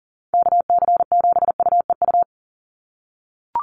Morse code, heard as a keyed tone, about 41 wpm, 700 Hz. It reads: KC7UEU
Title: USA Call Signs 40wpm